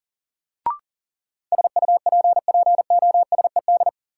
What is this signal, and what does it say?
Morse code, heard as a keyed tone, about 39 wpm, 700 Hz. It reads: SUPPOSED
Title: Sets of 3 Words - New Word Drill - 40wpm